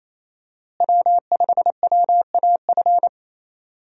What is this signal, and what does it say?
Morse code, heard as a keyed tone, about 28 wpm, 700 Hz. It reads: W5WAF
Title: USA Call Signs - 5 Characters - Repeated 3x - 28wpm